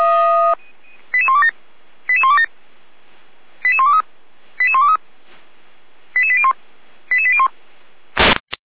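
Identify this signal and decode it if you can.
Five-tone selective call (ZVEI), heard as quick runs of five tones, tones of about 80 ms. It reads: 88127 88127 88123 88123 88801 88801
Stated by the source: ZVEI five-tone selective call received via SDR, recorded by Feuerwehr Orlamünde.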